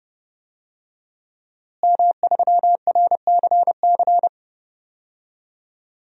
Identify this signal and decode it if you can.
Morse code, heard as a keyed tone, about 30 wpm, 700 Hz. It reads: M3RCC
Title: DXCC-100 Call Signs - 5 Characters - Repeated 2x - 30wpm